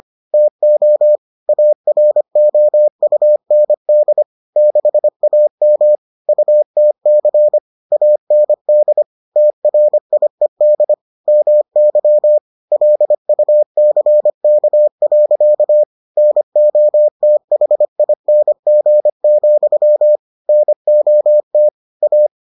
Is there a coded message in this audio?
Yes: Morse code from 0.3 to 22.3 s, heard as a keyed tone, about 25 wpm, 600 Hz. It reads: TO AROUND 6AM UTC AND TRIED MY LUCK. NOTHING, NOT A